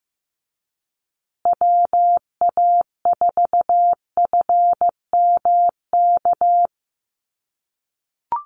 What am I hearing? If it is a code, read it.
Morse code, heard as a keyed tone, about 15 wpm, 700 Hz. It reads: WA4FMK
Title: USA Call Signs Repeated 3x 15wpm